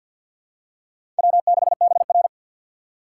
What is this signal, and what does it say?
Morse code, heard as a keyed tone, about 53 wpm, 700 Hz. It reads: W6BR
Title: USA Call Signs - 6x Character Spacing - 50wpm